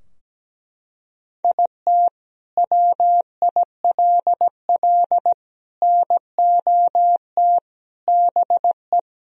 Morse code, heard as a keyed tone, about 17 wpm, 700 Hz. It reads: IT WILL NOT BE
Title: Sentences from Top 1000 Words - Review Part 2 - 17wpm